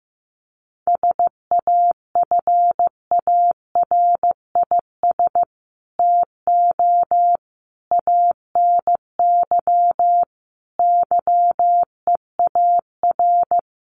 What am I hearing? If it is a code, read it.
Morse code, heard as a keyed tone, about 15 wpm, 700 Hz. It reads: SAFARIS TO ANY YEAR
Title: A Sound Of Thunder 15wpm